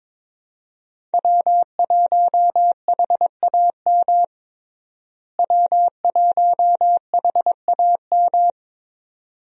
Morse code, heard as a keyed tone, about 22 wpm, 700 Hz. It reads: W1HAM W1HAM
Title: USA Call Signs - 5 Characters - Repeated 2x - 22wpm